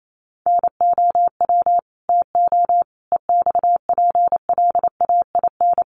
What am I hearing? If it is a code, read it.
Morse code, heard as a keyed tone, about 28 wpm, 700 Hz. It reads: NOW TO EXPLAIN
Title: Sentences from Top 1300 Words 28wpm